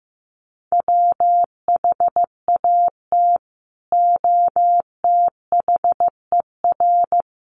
Morse code, heard as a keyed tone, about 14 wpm, 700 Hz. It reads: WHAT OTHER
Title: Sentences from Top 1400 Words 15wpm